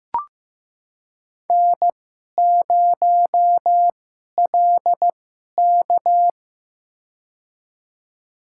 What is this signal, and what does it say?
Morse code, heard as a keyed tone, about 15 wpm, spaced as for 11 wpm, 700 Hz. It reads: N0LK
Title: USA Call Signs - 4 Characters - 2x Character Spacing - 15wpm